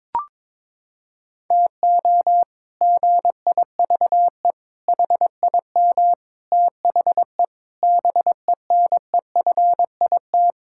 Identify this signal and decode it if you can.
Morse code, heard as a keyed tone, about 22 wpm, 700 Hz. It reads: TO GIVE HIM THE BENEFIT
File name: practice / 22wpm / Sets of 8 Words - 22wpm